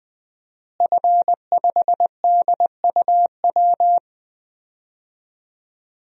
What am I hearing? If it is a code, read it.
Morse code, heard as a keyed tone, about 21 wpm, 700 Hz. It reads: F5DUW
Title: DXCC-100 Call Signs - 5 Characters - Repeated 2x - 20wpm